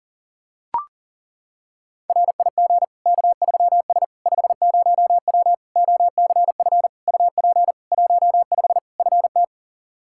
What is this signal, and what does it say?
Morse code, heard as a keyed tone, about 40 wpm, 700 Hz. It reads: RIG K3S 50W OCF UP 15 FT
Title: Practice QSO - Exchange 2 - Info - 40wpm